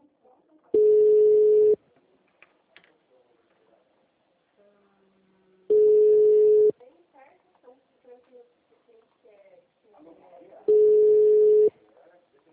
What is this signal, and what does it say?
Ringback tone (Europe, 425 Hz), with speech.
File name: out- -2013- - - .9816